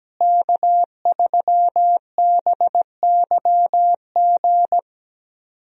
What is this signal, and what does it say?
Morse code, heard as a keyed tone, about 17 wpm, 700 Hz. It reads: K3BYG
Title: USA Call Signs - 5 Characters - Repeated 3x - 17wpm